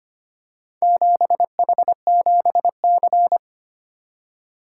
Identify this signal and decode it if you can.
Morse code, heard as a keyed tone, about 25 wpm, 700 Hz. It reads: 7H7C
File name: practice / 25wpm / Four Letter-Number Combinations - 25wpm